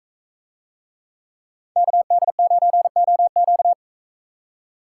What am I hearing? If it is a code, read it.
Morse code, heard as a keyed tone, about 39 wpm, 700 Hz. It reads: KD9OQ